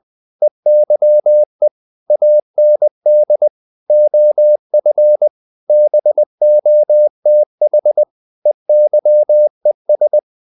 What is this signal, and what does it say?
Morse code, heard as a keyed tone, about 20 wpm, 600 Hz. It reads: EYE AND OF BOTH EYES
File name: the old man and the sea / the old man and the sea part 1 20wpm